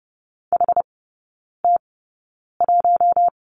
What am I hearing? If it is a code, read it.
Morse code, heard as a keyed tone, about 30 wpm, spaced as for 9 wpm, 700 Hz. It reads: HT1
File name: Lesson 18e - Letter m - Three Characters 30wpm